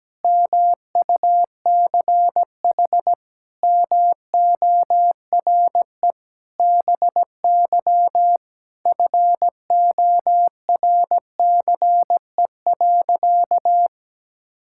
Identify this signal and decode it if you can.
Morse code, heard as a keyed tone, about 17 wpm, 700 Hz. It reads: MUCH MORE BY FORCE.
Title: Sentences from Top 1000 Words - Review Part 6 - 17wpm